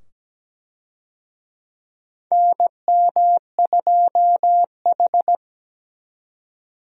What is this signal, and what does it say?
Morse code, heard as a keyed tone, about 17 wpm, 700 Hz. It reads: NM2H
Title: USA Call Signs - 4 Characters - 17wpm